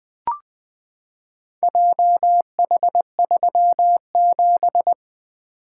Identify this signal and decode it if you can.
Morse code, heard as a keyed tone, about 20 wpm, 700 Hz. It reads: JH37
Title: Four Letter-Number Combinations - 20wpm